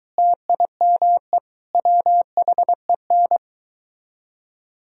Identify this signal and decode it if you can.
Morse code, heard as a keyed tone, about 22 wpm, 700 Hz. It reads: TIME WHEN